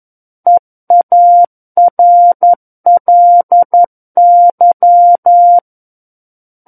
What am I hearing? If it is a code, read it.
Morse code, heard as a keyed tone, about 11 wpm, 700 Hz. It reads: EARLY
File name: 02-Chapter 2 - 11 WPM